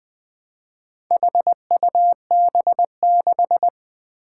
Morse code, heard as a keyed tone, about 21 wpm, 700 Hz. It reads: HUB6
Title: Four Challenging Letter-Number Combinations 20wpm